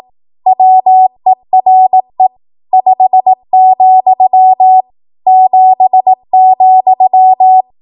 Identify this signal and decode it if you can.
Morse code, heard as a keyed tone, about 18 wpm, 750 Hz. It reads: WERE 5, 7,
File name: Archive / 18 WPM / 18WPM